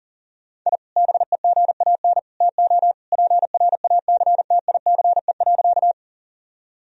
Morse code, heard as a keyed tone, about 40 wpm, 700 Hz. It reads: I BEGAN TO PRACTICE.
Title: Sentences from Top 1000 Words - Review Part 4 - 40wpm